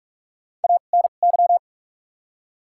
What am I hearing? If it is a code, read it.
Morse code, heard as a keyed tone, about 46 wpm, spaced as for 32 wpm, 700 Hz. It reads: ANY
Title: 2x Character Spacing - Top 200 Words - 45wpm